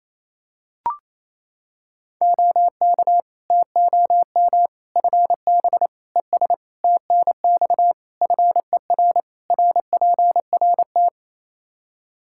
Morse code, heard as a keyed tone, about 27 wpm, 700 Hz. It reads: OK TOM FB ES TNX FER RPRT
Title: Practice QSO - Exchange 2 - Info - 28wpm